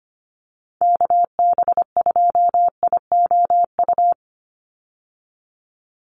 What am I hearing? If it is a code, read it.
Morse code, heard as a keyed tone, about 25 wpm, 700 Hz. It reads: KB2IOU
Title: USA Call Signs 25wpm